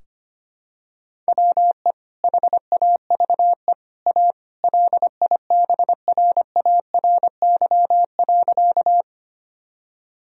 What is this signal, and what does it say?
Morse code, heard as a keyed tone, about 25 wpm, 700 Hz. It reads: WE HAVE A LIBRARY.
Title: Sentences from Top 1000 Words - Review Part 4 - 25wpm